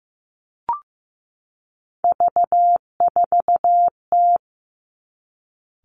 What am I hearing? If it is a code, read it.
Morse code, heard as a keyed tone, about 15 wpm, 700 Hz. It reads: V4T
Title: Three Challenging Letter-Number Combinations 15wpm